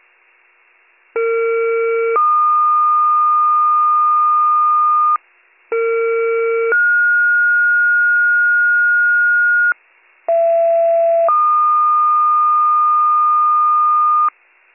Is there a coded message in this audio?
Yes: two-tone page from 1.2 to 14.3 s, heard as a one-second tone, then a three-second tone, about 460 then 1220 Hz, about 460 then 1430 Hz, about 670 then 1220 Hz.